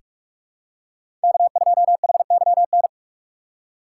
Morse code, heard as a keyed tone, about 46 wpm, 700 Hz. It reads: K2HYN
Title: USA Call Signs - 2x Character Spacing - 45wpm